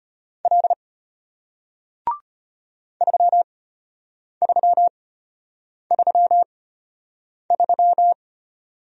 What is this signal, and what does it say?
Morse code, heard as a keyed tone, about 34 wpm, spaced as for 8 wpm, 700 Hz. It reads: L 3333